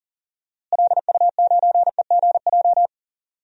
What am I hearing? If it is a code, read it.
Morse code, heard as a keyed tone, about 40 wpm, 700 Hz. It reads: LU9EGJ
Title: DXCC-100 Call Signs - 6 Characters - Repeated 2x - 40wpm